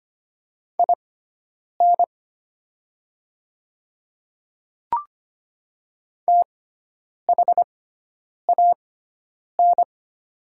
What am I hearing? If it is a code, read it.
Morse code, heard as a keyed tone, about 25 wpm, spaced as for 9 wpm, 700 Hz. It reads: IN THAN
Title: 6x Character Spacing - Top 100 Words - 25wpm